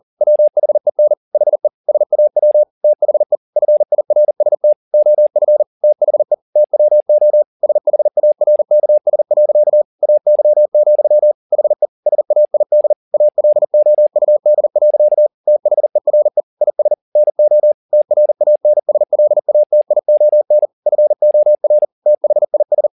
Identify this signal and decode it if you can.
Morse code, heard as a keyed tone, about 40 wpm, 600 Hz. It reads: WHEN HE SAW THE FIRST OF THE TWO SHARKS. AY, HE SAID ALOUD. THERE IS NO TRANSLATION FOR THIS
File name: the old man and the sea 2 40wpm